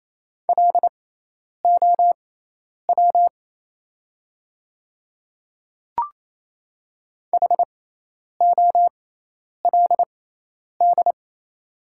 Morse code, heard as a keyed tone, about 28 wpm, spaced as for 10 wpm, 700 Hz. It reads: LOW HOLD
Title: 6x Character Spacing - Top 500 Words - 28wpm